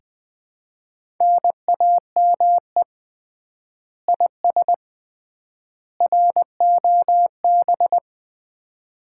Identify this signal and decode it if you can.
Morse code, heard as a keyed tone, about 20 wpm, 700 Hz. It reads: NAME IS ROB
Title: Common QSO Phrases - 3x Word-Spacing - 20wpm